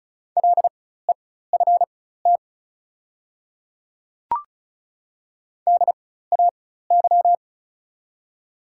Morse code, heard as a keyed tone, about 36 wpm, spaced as for 16 wpm, 700 Hz. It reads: LEFT DAY
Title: 4x Character Spacing - Top 300 Words - 35wpm